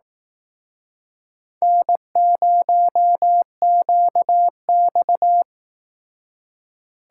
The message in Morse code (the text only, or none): N0QX